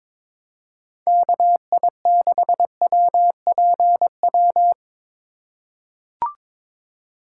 Morse code, heard as a keyed tone, about 22 wpm, 700 Hz. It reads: KI6WPW
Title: USA Call Signs - 3x Character Spacing - 22wpm